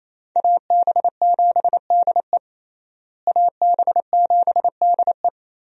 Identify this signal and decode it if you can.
Morse code, heard as a keyed tone, about 27 wpm, 700 Hz. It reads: AB7DE AB7DE
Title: USA Call Signs - 5 Characters - Repeated 3x - 28wpm